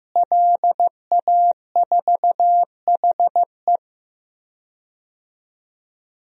Morse code, heard as a keyed tone, about 16 wpm, 700 Hz. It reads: LA4HE
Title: DXCC-100 Call Signs - 5 Characters - Repeated 3x - 15wpm